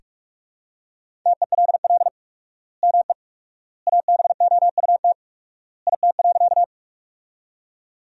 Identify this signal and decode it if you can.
Morse code, heard as a keyed tone, about 41 wpm, 700 Hz. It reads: TELL ME ABOUT IT.
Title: 4x Word-Spacing - Sentences from Top 200 Words - 45wpm